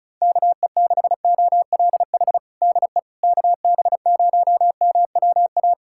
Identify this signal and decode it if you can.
Morse code, heard as a keyed tone, about 36 wpm, 700 Hz. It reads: KE6OLH DE KB0MWA
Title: USA Callsign DE Callsign - 6 Characters - 35wpm